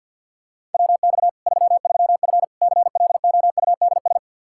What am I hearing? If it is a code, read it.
Morse code, heard as a keyed tone, about 53 wpm, 700 Hz. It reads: WX 33F CLOUDS
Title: Practice QSO - Exchange 2 - Info - 50wpm